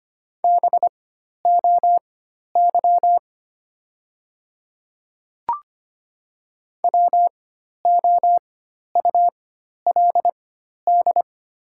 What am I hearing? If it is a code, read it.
Morse code, heard as a keyed tone, about 25 wpm, spaced as for 12 wpm, 700 Hz. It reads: BOY WOULD
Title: 4x Character Spacing - Top 300 Words - 25wpm